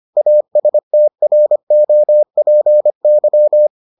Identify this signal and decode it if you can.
Morse code, heard as a keyed tone, about 25 wpm, 600 Hz. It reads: ASTROPY